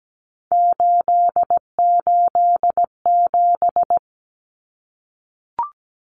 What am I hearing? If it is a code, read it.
Morse code, heard as a keyed tone, about 17 wpm, 700 Hz. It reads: 887
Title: Common Cut Numbers - 3-Digit 17wpm